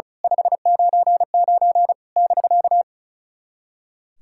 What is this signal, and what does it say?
Morse code, heard as a keyed tone, about 34 wpm, 690 Hz. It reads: 599 <BK>
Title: SOTA QSO Practice 35wpm